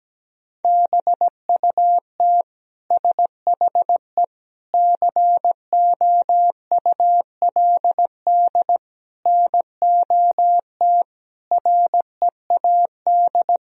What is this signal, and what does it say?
Morse code, heard as a keyed tone, about 17 wpm, 700 Hz. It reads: BUT SHE COULD NOT READ
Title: Sentences from Top 1000 Words - Review Part 4 - 17wpm